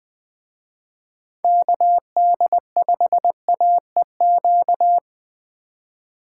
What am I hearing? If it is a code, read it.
Morse code, heard as a keyed tone, about 21 wpm, 700 Hz. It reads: KD5AEQ
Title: Speed Racing - USA 2x3 Call Signs 17wpm